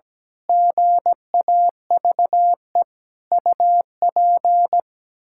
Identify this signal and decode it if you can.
Morse code, heard as a keyed tone, about 17 wpm, 700 Hz. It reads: GAVE UP
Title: Pangram Challenge v2 - 17wpm